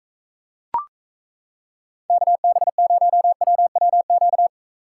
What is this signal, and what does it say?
Morse code, heard as a keyed tone, about 34 wpm, 700 Hz. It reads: KB0WWQ
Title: Speed Racing - USA Call Signs 28wpm